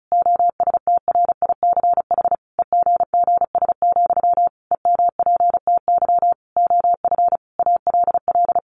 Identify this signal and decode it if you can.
Morse code, heard as a keyed tone, about 34 wpm, 700 Hz. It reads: OSTRICH EGGS, EMPTY OF ALL
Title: A Sound Of Thunder 35wpm